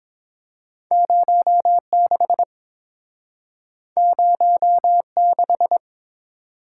Morse code, heard as a keyed tone, about 32 wpm, 700 Hz. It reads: 0 6 0 6